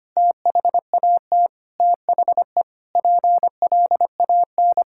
Morse code, heard as a keyed tone, about 25 wpm, 700 Hz. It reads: THAT THE PLAN
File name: practice / 25wpm / Sentences with 9-Word Phrases - 25wpm